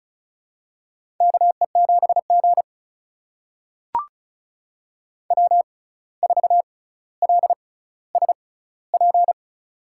Morse code, heard as a keyed tone, about 36 wpm, spaced as for 17 wpm, 700 Hz. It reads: KE7G W4LSP
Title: USA Call Signs - 6x Character Spacing - 35wpm